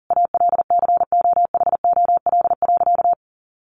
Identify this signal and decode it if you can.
Morse code, heard as a keyed tone, about 38 wpm, 700 Hz. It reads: ALCOHOL.
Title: Sentences from Top 1400 Words 40wpm